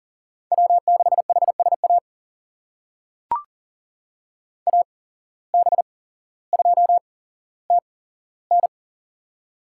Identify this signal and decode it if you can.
Morse code, heard as a keyed tone, about 40 wpm, spaced as for 17 wpm, 700 Hz. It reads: W6HSA AB2TN
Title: USA Call Signs - 8x Character Spacing - 40wpm